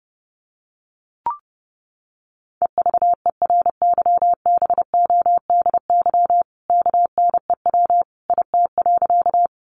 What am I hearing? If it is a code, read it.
Morse code, heard as a keyed tone, about 30 wpm, 700 Hz. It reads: EVERYBODY KNEW IT.
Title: Sentences from Top 1400 Words 30wpm